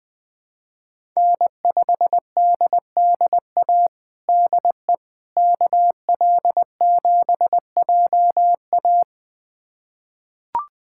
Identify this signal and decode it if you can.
Morse code, heard as a keyed tone, about 21 wpm, 700 Hz. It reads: N5DDA DE KL7JA
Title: USA Callsign DE Callsign - 5 Characters - 20wpm